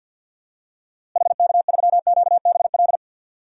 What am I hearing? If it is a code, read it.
Morse code, heard as a keyed tone, about 52 wpm, 700 Hz. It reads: HK3QBL
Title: DXCC-100 Call Signs - 6 Characters - Repeated 2x - 50wpm